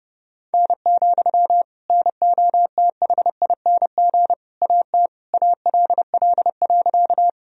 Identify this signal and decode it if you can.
Morse code, heard as a keyed tone, about 30 wpm, 700 Hz. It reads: N, NOTHING AT ALL.
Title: Sentences from Top 500 Words and Abbreviations 30wpm